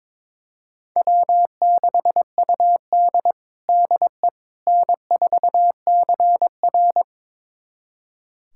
Morse code, heard as a keyed tone, about 23 wpm, 700 Hz. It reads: W6UD DE N4CR
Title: USA Callsign DE Callsign - 4 Characters - 22wpm